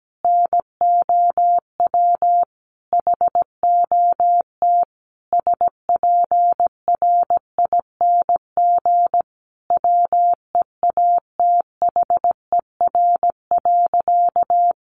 Morse code, heard as a keyed tone, about 17 wpm, 700 Hz. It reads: NOW HOT SPRING WEATHER.
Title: Sentences from Top 1200 Words 17wpm